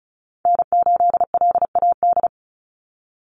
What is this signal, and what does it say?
Morse code, heard as a keyed tone, about 34 wpm, 700 Hz. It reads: N8LAD
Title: USA Call Signs 35wpm